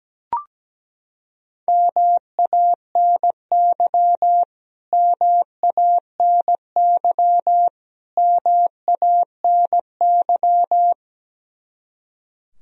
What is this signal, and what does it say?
Morse code, heard as a keyed tone, about 17 wpm, 700 Hz. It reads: MANY MANY MANY
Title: Top 100 Words Repeated 17wpm